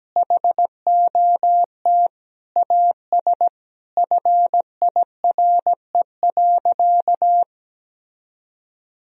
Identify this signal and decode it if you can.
Morse code, heard as a keyed tone, about 17 wpm, 700 Hz. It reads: HOT AS FIRE.
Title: Sentences from Top 1000 Words - Review Part 3 - 17wpm